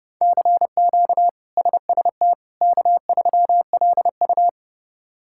Morse code, heard as a keyed tone, about 32 wpm, 700 Hz. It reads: CQ SST K3LU
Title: K1USN SST QSO Practice - 30wpm